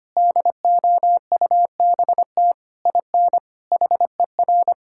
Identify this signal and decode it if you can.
Morse code, heard as a keyed tone, about 25 wpm, 700 Hz. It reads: DOUBT IN HER
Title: Sentences from Top 1600 Words 25wpm